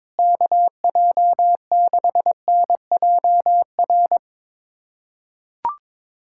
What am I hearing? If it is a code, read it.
Morse code, heard as a keyed tone, about 22 wpm, 700 Hz. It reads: KJ6NJR
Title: USA Callsign DE Callsign - 6 Characters - 22wpm